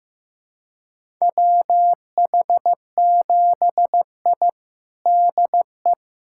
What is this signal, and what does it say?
Morse code, heard as a keyed tone, about 15 wpm, 700 Hz. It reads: WH7I DE
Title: USA Callsign DE Callsign - 4 Characters - 15wpm